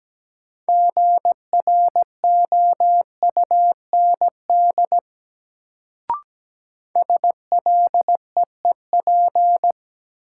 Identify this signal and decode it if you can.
Morse code, heard as a keyed tone, about 17 wpm, 700 Hz. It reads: GROUND SLEEP
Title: Sentences from Top 500 Words - New Word Drill - 17wpm